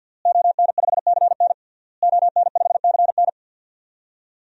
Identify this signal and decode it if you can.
Morse code, heard as a keyed tone, about 52 wpm, 700 Hz. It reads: ON5CN ON5CN
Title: DXCC-100 Call Signs - 5 Characters - Repeated 3x - 50wpm